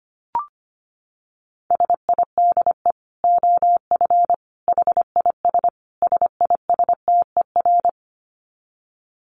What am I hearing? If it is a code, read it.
Morse code, heard as a keyed tone, about 25 wpm, 700 Hz. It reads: SIDE OF HIS SISTER
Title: Dit Challenge 25wpm